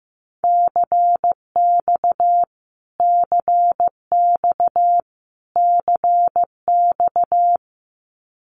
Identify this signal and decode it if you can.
Morse code, heard as a keyed tone, about 15 wpm, 700 Hz. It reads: CX CX CX
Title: Abbreviations Repeated 3x - 15wpm